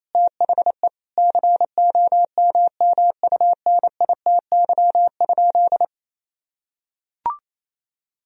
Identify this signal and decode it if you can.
Morse code, heard as a keyed tone, about 28 wpm, 700 Hz. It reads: THE COMMUNITY?
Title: Sentences from Top 1000 Words - Review Part 4 - 28wpm